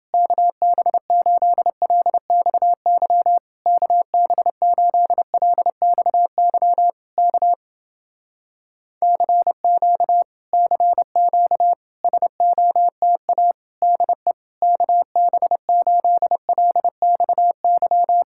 Morse code, heard as a keyed tone, about 29 wpm, 700 Hz. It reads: KB8LXY KB8LXY K CQ CQ SOTA DE KB8LXY